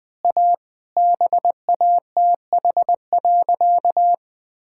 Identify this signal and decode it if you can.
Morse code, heard as a keyed tone, about 20 wpm, 700 Hz. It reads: A BATH.
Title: Easy Sentences with Hints - 20wpm